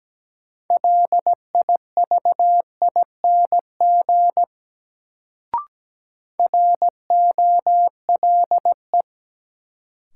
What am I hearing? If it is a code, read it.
Morse code, heard as a keyed tone, about 17 wpm, 700 Hz. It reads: LIVING ROLE
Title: Sentences from Top 1000 Words - Review Part 6 - New Word Drill - 17wpm